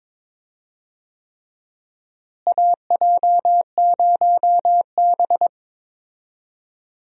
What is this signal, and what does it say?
Morse code, heard as a keyed tone, about 22 wpm, 700 Hz. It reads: AJ0B
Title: USA Call Signs - 4 Characters - Repeated 3x - 22wpm